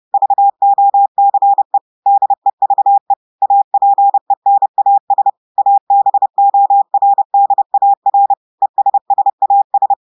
Morse code, heard as a keyed tone, about 30 wpm, 800 Hz. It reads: VOCE DEVE APENAS ABORDAR ESSAS
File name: audios / portugais / portugais texte en clair 2 vitesse 30 mots minutes